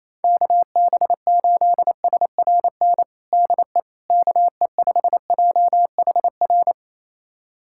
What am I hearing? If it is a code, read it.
Morse code, heard as a keyed tone, about 28 wpm, 700 Hz. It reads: KB8SRN DE KE5JHR
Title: USA Callsign DE Callsign - 6 Characters - 28wpm